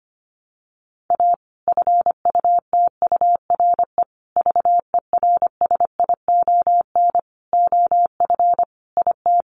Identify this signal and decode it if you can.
Morse code, heard as a keyed tone, about 25 wpm, 700 Hz. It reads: A FUTURE VERSION OF IT
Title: Sentences from Top 1400 Words 25wpm